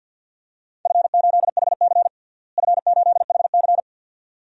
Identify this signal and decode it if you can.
Morse code, heard as a keyed tone, about 51 wpm, 700 Hz. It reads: F8HC F8HC
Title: DXCC-100 Call Signs - 4 Characters - Repeated 2x - 50wpm